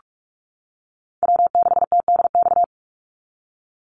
Morse code, heard as a keyed tone, about 44 wpm, 690 Hz. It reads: W6TDX
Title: NA SOTA Chaser Callsigns 45wpm